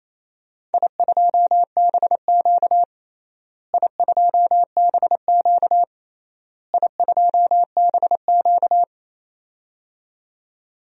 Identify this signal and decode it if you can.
Morse code, heard as a keyed tone, about 28 wpm, 700 Hz. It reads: I2BQ I2BQ I2BQ
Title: DXCC-100 Call Signs - 4 Characters - Repeated 3x - 28wpm